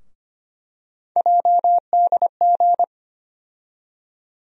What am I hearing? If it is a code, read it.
Morse code, heard as a keyed tone, about 25 wpm, 700 Hz. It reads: JDG
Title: USA Call Sign Postfixes - 3 Letters 25wpm